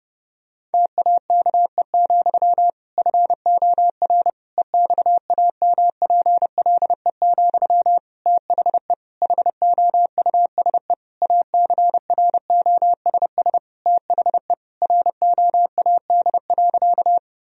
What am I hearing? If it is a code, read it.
Morse code, heard as a keyed tone, about 30 wpm, 700 Hz. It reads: TAKE, FOR EXAMPLE, THE HOUSE ACROSS THE ROAD.
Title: Sentences from Top 1000 Words - Review Part 1 - 30wpm